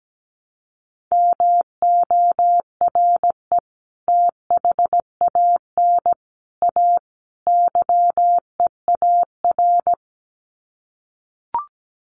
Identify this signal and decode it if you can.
Morse code, heard as a keyed tone, about 17 wpm, 700 Hz. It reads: MORE THAN A YEAR
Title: Sets of 4 Words 17wpm